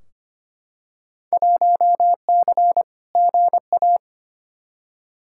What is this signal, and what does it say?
Morse code, heard as a keyed tone, about 25 wpm, 700 Hz. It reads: 1C GA
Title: Field Day Exchanges Encore - 25wpm